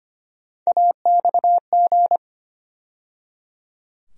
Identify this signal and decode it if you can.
Morse code, heard as a keyed tone, about 25 wpm, 700 Hz. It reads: AXG
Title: USA Call Sign Postfixes - 3 Letters 25wpm